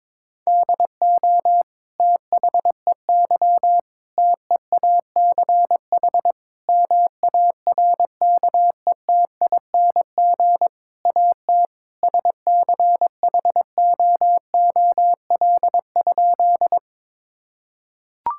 Morse code, heard as a keyed tone, about 22 wpm, 700 Hz. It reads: DO THEY TEACH MARKETING AT SCHOOL?
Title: Sentences from Top 1000 Words - Review Part 5 - 22wpm